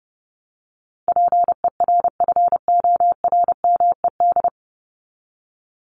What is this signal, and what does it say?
Morse code, heard as a keyed tone, about 30 wpm, 700 Hz. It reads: PERFORMED
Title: ED Suffix - Top 100 Words with Base Words 30wpm